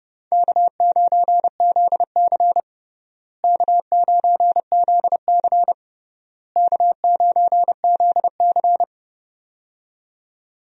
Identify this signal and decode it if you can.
Morse code, heard as a keyed tone, about 30 wpm, 700 Hz. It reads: K9ZC K9ZC K9ZC
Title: USA Call Signs - 4 Characters - Repeated 3x - 30wpm